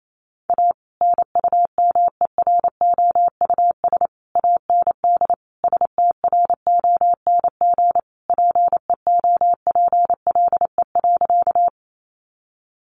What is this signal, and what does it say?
Morse code, heard as a keyed tone, about 28 wpm, 700 Hz. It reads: A NUMEROUS AND STRONG PEOPLE.
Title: Sentences from Top 1400 Words 28wpm